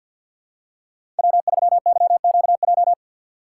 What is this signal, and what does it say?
Morse code, heard as a keyed tone, about 53 wpm, 700 Hz. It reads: W3YQJ
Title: Code Groups - 5 Letter:Numbers - 50wpm